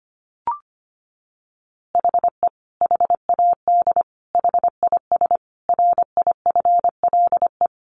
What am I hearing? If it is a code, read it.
Morse code, heard as a keyed tone, about 25 wpm, 700 Hz. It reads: HE HAD HIS RIFLE
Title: A Sound Of Thunder 25wpm